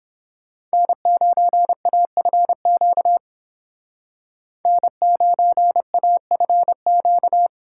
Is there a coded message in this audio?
Yes: Morse code from 0.7 to 7.5 s, heard as a keyed tone, about 29 wpm, 700 Hz. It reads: N9AFQ N9AFQ